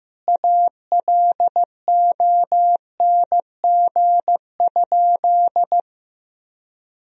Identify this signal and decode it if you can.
Morse code, heard as a keyed tone, about 16 wpm, 700 Hz. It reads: ALONG?
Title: Sentences from Top 1000 Words - Review Part 3 - 15wpm